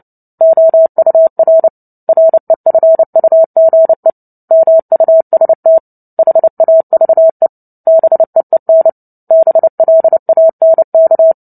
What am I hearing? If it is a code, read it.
Morse code, heard as a keyed tone, about 30 wpm, 650 Hz. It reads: OUR REFUGE MUST HAVE BEEN BLANK
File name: Book Two / 03-Chapter 3 - 29 WPM